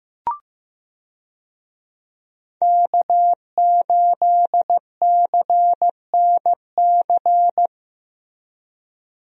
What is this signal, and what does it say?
Morse code, heard as a keyed tone, about 15 wpm, 700 Hz. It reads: K8CNC
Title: USA Call Signs - 5 Characters - Repeated 3x - 15wpm